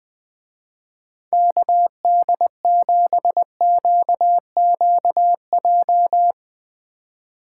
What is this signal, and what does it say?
Morse code, heard as a keyed tone, about 20 wpm, 700 Hz. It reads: KD7QQJ
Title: Speed Racing - USA Call Signs 17wpm